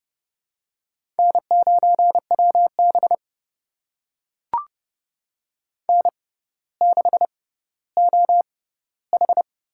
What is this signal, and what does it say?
Morse code, heard as a keyed tone, about 31 wpm, spaced as for 16 wpm, 700 Hz. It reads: N9WB N6OH
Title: USA Call Signs - 4 Characters - 6x Character Spacing - 30wpm